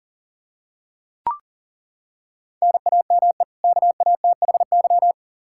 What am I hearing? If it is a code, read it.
Morse code, heard as a keyed tone, about 40 wpm, 700 Hz. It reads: NAME KATHY
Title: Practice QSO - Exchange 1 - Beginning - 40wpm